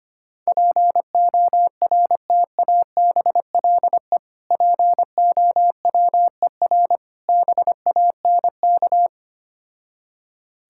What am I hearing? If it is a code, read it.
Morse code, heard as a keyed tone, about 25 wpm, 700 Hz. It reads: PORTABLE POWER BANK
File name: practice / 25wpm / Difficult Words with Hints - 25wpm